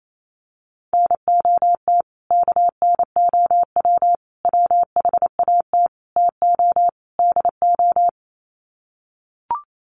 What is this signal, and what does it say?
Morse code, heard as a keyed tone, about 28 wpm, 700 Hz. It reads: NOT KNOW WHAT TO DO
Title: Sets of 5 Words 28wpm